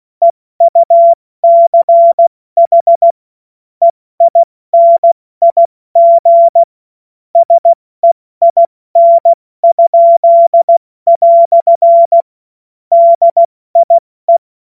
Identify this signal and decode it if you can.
Morse code, heard as a keyed tone, about 16 wpm, spaced as for 14 wpm, 680 Hz. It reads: EUCH EINIG SEIN?" DIE
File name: Morsekurs / Witze / Witz 17. Januar 2023 70 F80